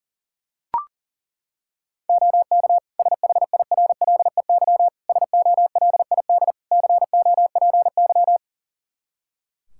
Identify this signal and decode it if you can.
Morse code, heard as a keyed tone, about 40 wpm, 700 Hz. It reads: OK SHIRLEY SOLID COPY
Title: Practice QSO - Exchange 3 - More Info - 40wpm